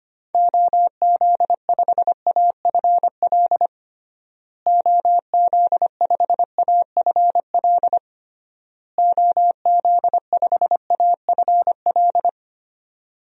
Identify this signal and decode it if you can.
Morse code, heard as a keyed tone, about 25 wpm, 700 Hz. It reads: OZ5AFL OZ5AFL OZ5AFL
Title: DXCC-100 Call Signs - 6 Characters - Repeated 3x - 25wpm